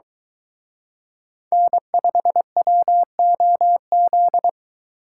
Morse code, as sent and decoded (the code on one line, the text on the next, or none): -. ..... .-- --- --..
N5WOZ